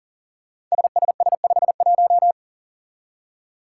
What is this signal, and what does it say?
Morse code, heard as a keyed tone, about 42 wpm, 700 Hz. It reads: SSS51
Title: Five Challenging Letter-Number Combinations 40wpm